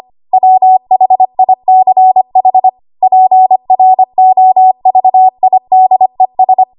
Morse code, heard as a keyed tone, about 25 wpm, 750 Hz. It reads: WHICH PROVIDES
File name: Archive / 25 WPM / 25WPM